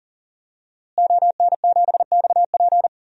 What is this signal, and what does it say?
Morse code, heard as a keyed tone, about 40 wpm, 700 Hz. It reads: ON7XP
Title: DXCC-100 Call Signs - 5 Characters - Repeated 2x - 40wpm